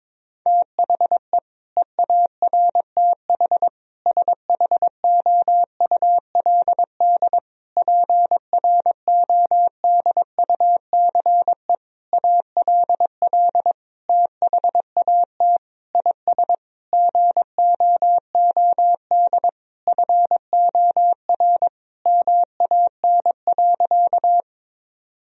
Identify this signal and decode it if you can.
Morse code, heard as a keyed tone, about 21 wpm, 700 Hz. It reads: THE EARTH SHOULD PRODUCE ALL THAT IS GOOD FOR MAN.
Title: Sentences from Top 1000 Words - Review Part 3 - 22wpm